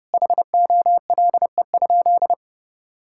Morse code, heard as a keyed tone, about 30 wpm, 700 Hz. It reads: HOLE?
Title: Sentences from Top 1600 Words 30wpm